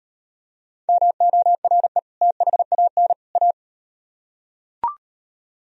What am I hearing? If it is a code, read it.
Morse code, heard as a keyed tone, about 38 wpm, 700 Hz. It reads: MORE THAN A